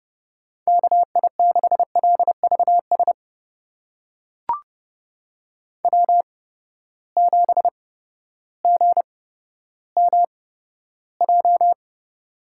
Morse code, heard as a keyed tone, about 30 wpm, spaced as for 14 wpm, 700 Hz. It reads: KI6LVS W7GMJ
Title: USA Call Signs - 8x Character Spacing - 30wpm